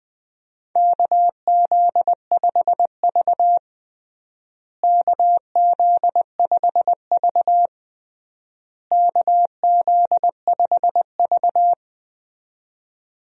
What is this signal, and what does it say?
Morse code, heard as a keyed tone, about 20 wpm, 700 Hz. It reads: KZ5V KZ5V KZ5V
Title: USA Call Signs - 4 Characters - Repeated 3x - 20wpm